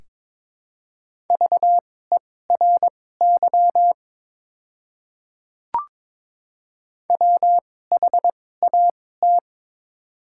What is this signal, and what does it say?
Morse code, heard as a keyed tone, about 22 wpm, spaced as for 16 wpm, 700 Hz. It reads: VERY WHAT
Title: 2x Character Spacing - Top 200 Words - 22wpm